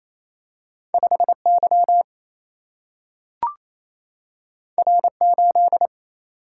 Morse code, heard as a keyed tone, about 28 wpm, 700 Hz. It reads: 5 Y R 8
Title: Single-Letter Single-Number 28wpm